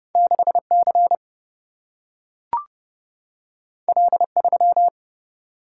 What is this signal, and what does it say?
Morse code, heard as a keyed tone, about 30 wpm, 700 Hz. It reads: 6 C L 3
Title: Single-Letter Single-Number 30wpm